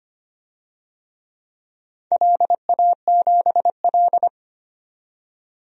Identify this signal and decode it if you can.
Morse code, heard as a keyed tone, about 25 wpm, 700 Hz. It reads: LA7L
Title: DXCC-100 Call Signs - 4 Characters - 25wpm